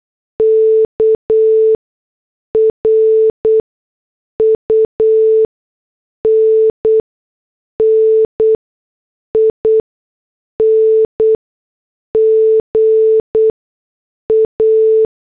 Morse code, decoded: K RUNNING A